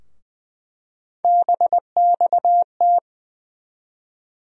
Morse code, heard as a keyed tone, about 20 wpm, 700 Hz. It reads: BXT
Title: USA Call Sign Postfixes - 3 Letters 20wpm